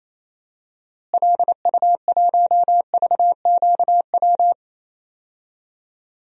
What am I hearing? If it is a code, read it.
Morse code, heard as a keyed tone, about 28 wpm, 700 Hz. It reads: LU1VQW
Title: DXCC-100 Call Signs - 6 Characters - Repeated 2x - 28wpm